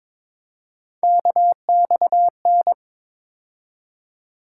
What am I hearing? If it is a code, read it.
Morse code, heard as a keyed tone, about 22 wpm, 700 Hz. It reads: KXN
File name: practice / 22wpm / Three Letter-Number Combinations - 22wpm